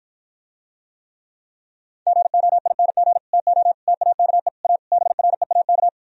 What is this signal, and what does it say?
Morse code, heard as a keyed tone, about 52 wpm, 700 Hz. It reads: GOING TO TAKE A BREAK